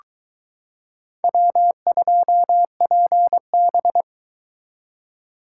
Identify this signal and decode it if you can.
Morse code, heard as a keyed tone, about 22 wpm, 700 Hz. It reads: W2PB